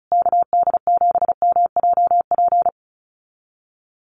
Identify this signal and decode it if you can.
Morse code, heard as a keyed tone, about 34 wpm, 700 Hz. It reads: KD7MJP
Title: USA Call Signs Repeated 3x 35wpm